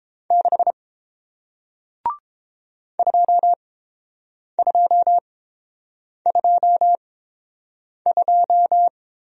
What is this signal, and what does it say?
Morse code, heard as a keyed tone, about 30 wpm, spaced as for 7 wpm, 700 Hz. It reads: 6 2222